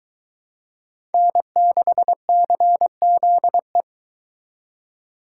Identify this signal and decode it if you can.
Morse code, heard as a keyed tone, about 23 wpm, 700 Hz. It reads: N6CZE